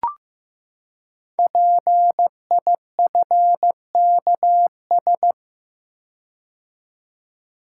Morse code, heard as a keyed tone, about 15 wpm, 700 Hz. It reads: PIFKS
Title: Code Groups - 5 Letters - 15wpm